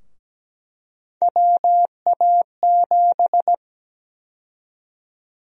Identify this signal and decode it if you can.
Morse code, heard as a keyed tone, about 17 wpm, 700 Hz. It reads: WA7
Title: USA Call Sign Prefixes - 17wpm